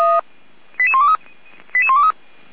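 Five-tone selective call (ZVEI), heard as quick runs of five tones, tones of about 80 ms. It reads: 88123 88123
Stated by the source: ZVEI five-tone selective call received on an SDR, recorded by the feuerwehr-orlamuende fire brigade.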